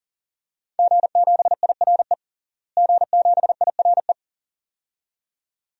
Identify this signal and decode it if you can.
Morse code, heard as a keyed tone, about 40 wpm, 700 Hz. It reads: G7IRE G7IRE
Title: DXCC-100 Call Signs - 5 Characters - Repeated 2x - 40wpm